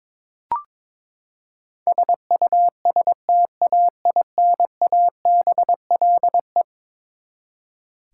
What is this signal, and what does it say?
Morse code, heard as a keyed tone, about 22 wpm, 700 Hz. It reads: SUSTAINABLE
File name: practice / 22wpm / Eleven Letter Words - 22wpm